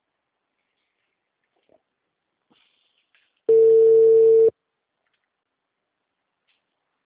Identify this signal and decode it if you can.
Ringback tone (China, 450 Hz).